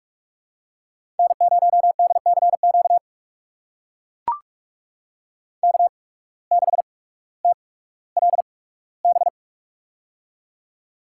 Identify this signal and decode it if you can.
Morse code, heard as a keyed tone, about 42 wpm, spaced as for 19 wpm, 700 Hz. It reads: N0DCQ K6TLB
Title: USA Call Signs - 5 Characters - 8x Character Spacing - 45wpm